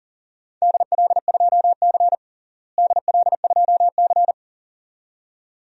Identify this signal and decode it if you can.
Morse code, heard as a keyed tone, about 40 wpm, 700 Hz. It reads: DL2C DL2C
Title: DXCC-100 Call Signs - 4 Characters - Repeated 2x - 40wpm